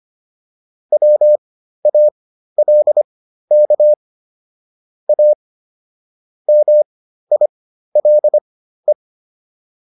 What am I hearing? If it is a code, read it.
Morse code, heard as a keyed tone, about 25 wpm, spaced as for 13 wpm, 600 Hz. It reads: WALK A MILE